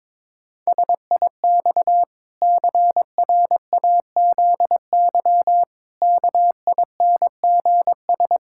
Morse code, heard as a keyed tone, about 22 wpm, 700 Hz. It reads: SIX CRAZY KINGS
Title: Pangram Challenge v2 - 22wpm